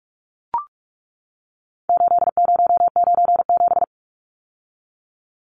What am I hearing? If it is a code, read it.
Morse code, heard as a keyed tone, about 44 wpm, 700 Hz. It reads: 8097
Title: Numbers - 4-Digit 45wpm